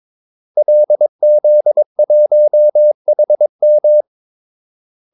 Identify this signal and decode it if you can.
Morse code, heard as a keyed tone, about 22 wpm, 600 Hz. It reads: LZ1HM
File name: Callsigns-22wpm